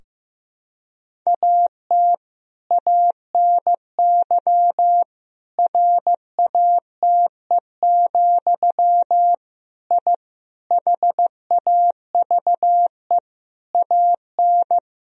Morse code, heard as a keyed tone, about 15 wpm, 700 Hz. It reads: AT ANY RATE, I HAVE AN
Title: Sentences from Top 1000 Words - Review Part 4 - 15wpm